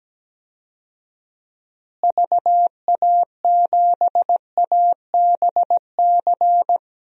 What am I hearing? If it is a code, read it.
Morse code, heard as a keyed tone, about 17 wpm, 700 Hz. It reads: VA7ABC
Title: DXCC-100 Call Signs - 6 Characters - Repeated 2x - 17wpm